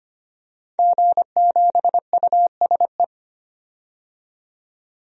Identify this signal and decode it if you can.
Morse code, heard as a keyed tone, about 25 wpm, 700 Hz. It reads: G7USE
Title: DXCC-100 Call Signs - 5 Characters - 25wpm